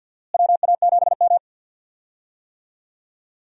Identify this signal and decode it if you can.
Morse code, heard as a keyed tone, about 47 wpm, 700 Hz. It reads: WA7M
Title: USA Call Signs - 4 Characters - Repeated 2x - 50wpm